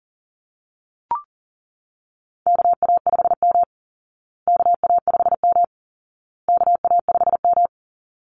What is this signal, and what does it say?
Morse code, heard as a keyed tone, about 37 wpm, 700 Hz. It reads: KA5M KA5M KA5M
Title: Call Signs Repeated 2x - 40wpm